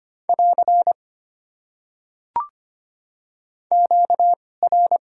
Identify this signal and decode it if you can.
Morse code, heard as a keyed tone, about 25 wpm, spaced as for 18 wpm, 700 Hz. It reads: <AR> QR
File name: Top 90 QSO Elements - 2x Character Spacing - 25wpm